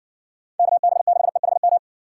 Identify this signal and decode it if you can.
Morse code, heard as a keyed tone, about 60 wpm, 700 Hz. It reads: KB6EHM